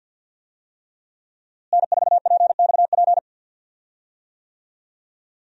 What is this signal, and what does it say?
Morse code, heard as a keyed tone, about 51 wpm, 700 Hz. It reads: N4PXP
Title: USA Call Signs - 5 Characters - Repeated 3x - 50wpm